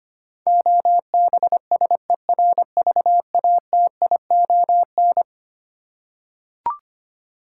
Morse code, heard as a keyed tone, about 25 wpm, 700 Hz. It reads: OBSERVATION
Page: morse-code-ninja-practice-25wpm